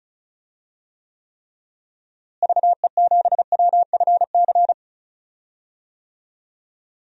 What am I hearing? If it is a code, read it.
Morse code, heard as a keyed tone, about 36 wpm, 700 Hz. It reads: VE7WFC
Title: DXCC-100 Call Signs - 6 Characters - Repeated 3x - 35wpm